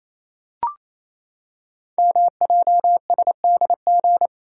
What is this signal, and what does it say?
Morse code, heard as a keyed tone, about 27 wpm, 700 Hz. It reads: MJSDG
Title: Code Groups - 5 Letters - 28wpm